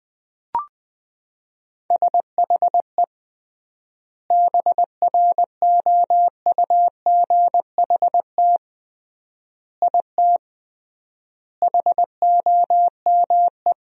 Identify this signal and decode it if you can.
Morse code, heard as a keyed tone, about 21 wpm, 700 Hz. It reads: SHE BROUGHT IT HOME
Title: 3x Word-Spacing - Sentences from Top 500 Words - 20wpm